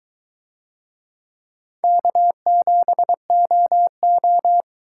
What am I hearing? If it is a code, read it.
Morse code, heard as a keyed tone, about 22 wpm, 700 Hz. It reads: K7OO